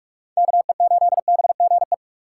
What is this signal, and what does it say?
Morse code, heard as a keyed tone, about 46 wpm, 700 Hz. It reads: KE8BGE
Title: USA Call Signs - 6 Characters - 3x Character Spacing - 45wpm